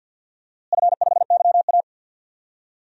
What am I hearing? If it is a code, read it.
Morse code, heard as a keyed tone, about 50 wpm, 700 Hz. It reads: F5YA
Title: DXCC-100 Call Signs - 4 Characters - Repeated 3x - 50wpm